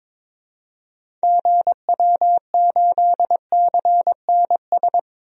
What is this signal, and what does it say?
Morse code, heard as a keyed tone, about 22 wpm, 700 Hz. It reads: GW8CNS
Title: DXCC-100 Call Signs - 6 Characters - Repeated 2x - 22wpm